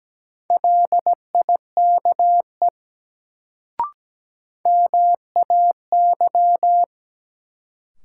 Morse code, heard as a keyed tone, about 17 wpm, 700 Hz. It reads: LIKE MAY
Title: Sets of 2 Words - New Word Drill - 17wpm